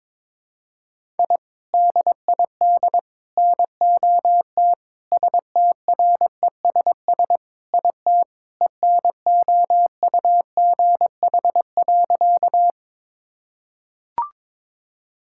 Morse code, heard as a keyed tone, about 22 wpm, 700 Hz. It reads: I DID NOT STRESS IT ENOUGH.
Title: Sentences from Top 1000 Words - Review Part 6 - 22wpm